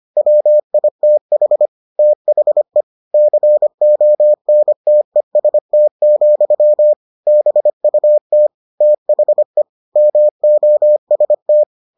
Morse code, heard as a keyed tone, about 25 wpm, 600 Hz. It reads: WITH THE CONTEST, BUT THE MOST